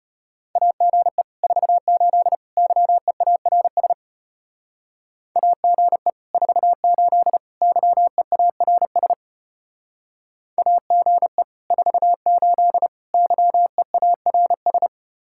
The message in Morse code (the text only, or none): AGE 48 YEARS AGE 48 YEARS AGE 48 YEARS